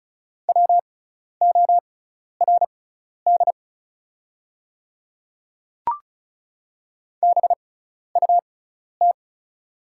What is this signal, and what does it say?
Morse code, heard as a keyed tone, about 34 wpm, spaced as for 12 wpm, 700 Hz. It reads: WORD BUT
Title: 6x Character Spacing - Top 100 Words - 35wpm